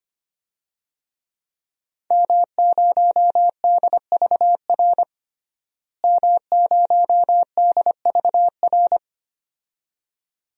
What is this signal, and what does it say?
Morse code, heard as a keyed tone, about 25 wpm, 700 Hz. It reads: M0DVR M0DVR
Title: DXCC-100 Call Signs - 5 Characters - Repeated 2x - 25wpm